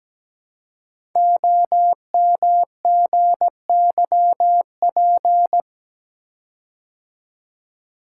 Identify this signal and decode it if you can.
Morse code, heard as a keyed tone, about 17 wpm, 700 Hz. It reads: OMGYP
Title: Code Groups - 5 Letters - 17wpm